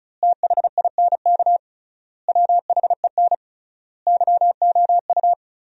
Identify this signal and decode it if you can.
Morse code, heard as a keyed tone, about 36 wpm, 700 Hz. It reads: THINK WHEN YOU
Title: 3x Word-Spacing - Sentences from Top 300 Words - 35wpm